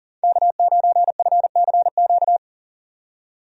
Morse code, heard as a keyed tone, about 38 wpm, 700 Hz. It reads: K9FCQ
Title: USA Call Signs - 5 Characters - 40wpm